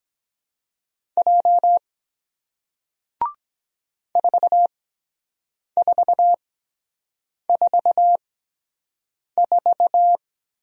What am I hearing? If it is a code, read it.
Morse code, heard as a keyed tone, about 23 wpm, spaced as for 7 wpm, 700 Hz. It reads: J 4444